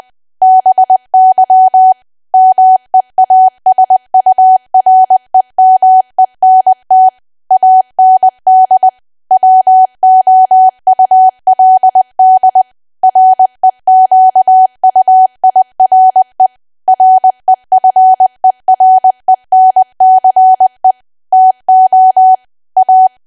Morse code, heard as a keyed tone, about 20 wpm, 750 Hz. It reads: BY MEASUREMENT AND WOULD REQUIRE REFERENCE TO A
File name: cwa / W1AW-Level3 / 20WPM